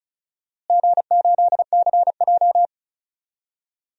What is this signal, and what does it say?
Morse code, heard as a keyed tone, about 34 wpm, 700 Hz. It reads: G8CJ
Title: DXCC-100 Call Signs - 4 Characters - Repeated 2x - 35wpm